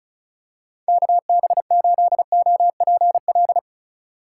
Morse code, heard as a keyed tone, about 34 wpm, 700 Hz. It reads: KB8OPL
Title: USA Call Signs - 6 Characters - 4x Character Spacing - 35wpm